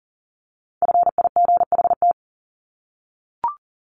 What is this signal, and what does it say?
Morse code, heard as a keyed tone, about 40 wpm, 700 Hz. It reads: FIGHT
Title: ING Suffix - Top 100 Words with Base Words 40wpm